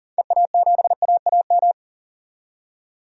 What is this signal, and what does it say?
Morse code, heard as a keyed tone, about 40 wpm, 700 Hz. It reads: EA7AAM
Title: DXCC-100 Call Signs - 6 Characters - 40wpm